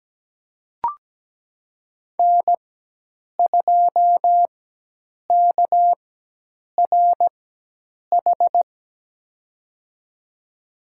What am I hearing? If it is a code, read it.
Morse code, heard as a keyed tone, about 17 wpm, spaced as for 8 wpm, 700 Hz. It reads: N2KRH
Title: USA Call Signs - 4x Character Spacing - 17wpm